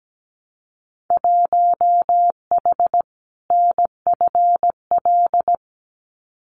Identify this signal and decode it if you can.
Morse code, heard as a keyed tone, about 17 wpm, 700 Hz. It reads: 1H NFL
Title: Winter Field Day Exchanges 17wpm